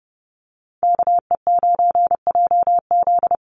Morse code, heard as a keyed tone, about 28 wpm, 700 Hz. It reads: KE9JZ
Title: USA Call Signs 30wpm